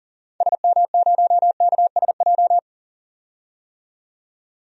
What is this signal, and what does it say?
Morse code, heard as a keyed tone, about 40 wpm, 700 Hz. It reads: SM0KSJ
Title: DXCC-100 Call Signs - 6 Characters - 40wpm